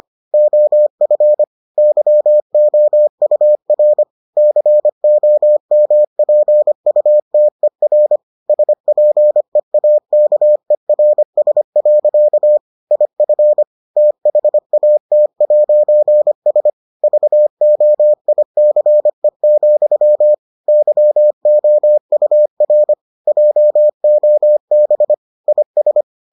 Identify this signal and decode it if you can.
Morse code, heard as a keyed tone, about 25 wpm, 600 Hz. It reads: OF YOUR COMPUTER SPEAKERS. IF THAT'S VOICE, YOUR JOB IS